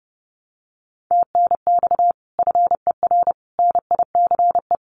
Morse code, heard as a keyed tone, about 30 wpm, 700 Hz. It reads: TNX FER NICE
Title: Common QSO Phrases Encore 30wpm